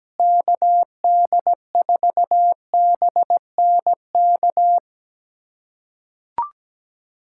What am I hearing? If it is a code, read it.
Morse code, heard as a keyed tone, about 17 wpm, 700 Hz. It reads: KD4BNK
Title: USA Callsign DE Callsign - 6 Characters - 17wpm